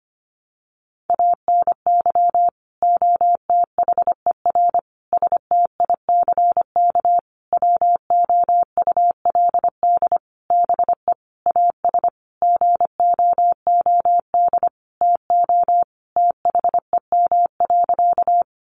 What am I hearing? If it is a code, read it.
Morse code, heard as a keyed tone, about 25 wpm, 700 Hz. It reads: ANY OTHER STICK WOULD BE AS GOOD TO THEM.
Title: Sentences from Top 1200 Words 25wpm